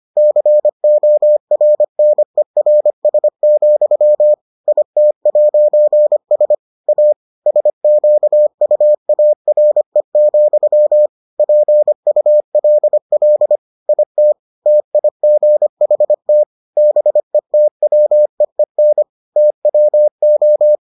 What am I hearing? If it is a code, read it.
Morse code, heard as a keyed tone, about 25 wpm, 600 Hz. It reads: CORNERS, IT'S A SQUARE, PULL IT TIGHT BETWEEN TWO